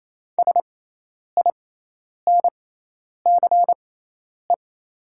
Morse code, heard as a keyed tone, about 28 wpm, spaced as for 10 wpm, 690 Hz. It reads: SINCE
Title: 6x Character Spacing - Top 300 Words - 28wpm